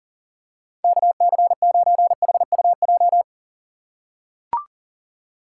Morse code, heard as a keyed tone, about 40 wpm, 700 Hz. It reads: KC9HUJ
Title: USA Call Signs - 6x Character Spacing - 40wpm